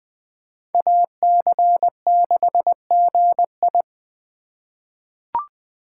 Morse code, heard as a keyed tone, about 20 wpm, 700 Hz. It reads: AC6GI
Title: USA Call Signs - 5 Characters - 4x Character Spacing - 20wpm